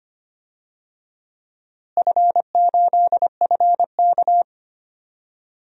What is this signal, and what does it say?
Morse code, heard as a keyed tone, about 25 wpm, 700 Hz. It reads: F8FK
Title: DXCC-100 Call Signs - 4 Characters - Repeated 2x - 25wpm